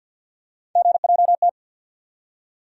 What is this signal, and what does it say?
Morse code, heard as a keyed tone, about 53 wpm, 700 Hz. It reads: GJT
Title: USA Call Sign Postfixes - 3 Letters 50wpm